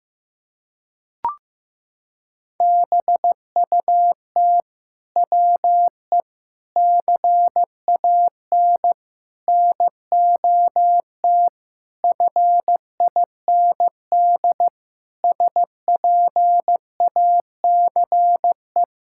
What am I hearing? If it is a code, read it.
Morse code, heard as a keyed tone, about 15 wpm, 700 Hz. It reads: BUT WE CAN NOT FIND SPACE
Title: Sentences from Top 1000 Words - Review Part 5 - 15wpm